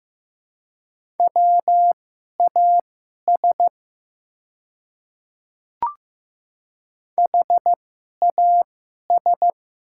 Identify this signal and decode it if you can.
Morse code, heard as a keyed tone, about 15 wpm, spaced as for 11 wpm, 700 Hz. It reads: WAS HAS
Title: 2x Character Spacing - Top 100 Words - 15wpm